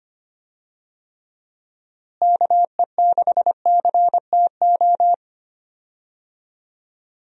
Morse code, heard as a keyed tone, about 25 wpm, 700 Hz. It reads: KE6CTO
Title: USA Call Signs - 6 Characters - Repeated 2x - 25wpm